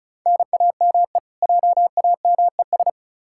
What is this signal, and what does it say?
Morse code, heard as a keyed tone, about 36 wpm, 700 Hz. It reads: NAME JAMES
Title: Practice QSO - Exchange 1 - Beginning - 35wpm